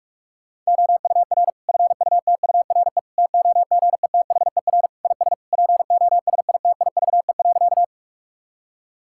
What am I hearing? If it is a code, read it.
Morse code, heard as a keyed tone, about 41 wpm, 700 Hz. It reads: OUR FUTURE TOGETHER IS POSITIVE.
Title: Sentences from Top 1000 Words - Review Part 5 - 45wpm